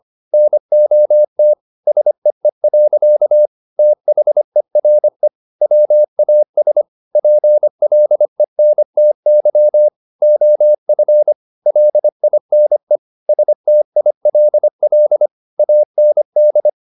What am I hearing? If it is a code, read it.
Morse code, heard as a keyed tone, about 25 wpm, 600 Hz. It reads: NOT SEE. THERE WAS PLENTY OF LINE STILL AND